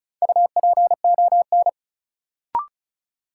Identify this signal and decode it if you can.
Morse code, heard as a keyed tone, about 36 wpm, 700 Hz. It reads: UPON
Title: Sentences from Top 1000 Words - Review Part 6 - New Word Drill - 35wpm